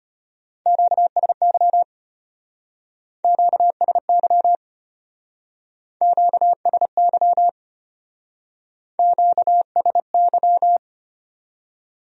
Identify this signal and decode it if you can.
Morse code, heard as a keyed tone, about 31 wpm, 700 Hz. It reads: QSY QSY QSY QSY
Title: Speed-Racing Q-Codes - 25wpm